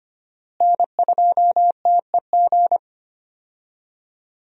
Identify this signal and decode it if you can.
Morse code, heard as a keyed tone, about 25 wpm, 700 Hz. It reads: N2TEG
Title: USA Call Signs - 5 Characters - Repeated 2x - 25wpm